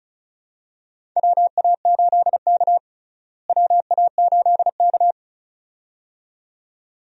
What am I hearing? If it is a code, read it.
Morse code, heard as a keyed tone, about 36 wpm, 700 Hz. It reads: WA8K WA8K
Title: USA Call Signs - 4 Characters - Repeated 2x - 35wpm